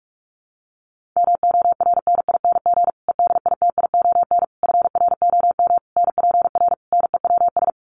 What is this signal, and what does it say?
Morse code, heard as a keyed tone, about 44 wpm, 700 Hz. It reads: MORNING EDITION FROM NPR NEWS
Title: Sets of 5 Words Encore 45wpm